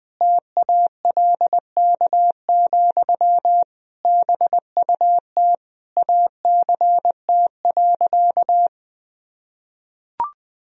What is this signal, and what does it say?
Morse code, heard as a keyed tone, about 20 wpm, 700 Hz. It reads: TALK, BUT ACT.
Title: Sentences from Top 1000 Words - Review Part 1 - 20wpm